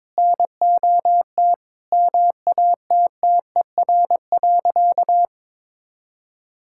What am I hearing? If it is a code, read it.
Morse code, heard as a keyed tone, about 21 wpm, 700 Hz. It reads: NOT MATTER.
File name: practice / 22wpm / Sentences with 3-Word Phrases - 22wpm